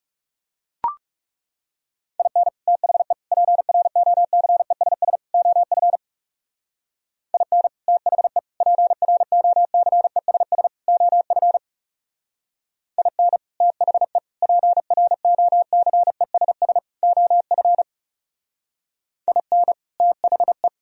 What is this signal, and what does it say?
Morse code, heard as a keyed tone, about 37 wpm, 700 Hz. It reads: IN THE PROCESS OF IN THE PROCESS OF IN THE PROCESS OF IN THE
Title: Speed-Racing - Sets of 4 Words 30wpm